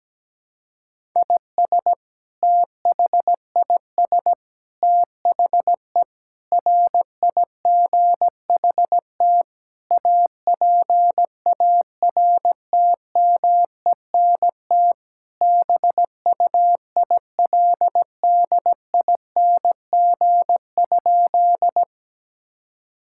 Morse code, heard as a keyed tone, about 17 wpm, 700 Hz. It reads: IS THIS THE RIGHT APARTMENT BUILDING?
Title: Sentences from Top 1000 Words - Review Part 6 - 17wpm